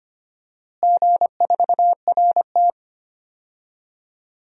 Morse code, heard as a keyed tone, about 25 wpm, 700 Hz. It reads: G4RT
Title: DXCC-100 Call Signs - 4 Characters - Repeated 2x - 25wpm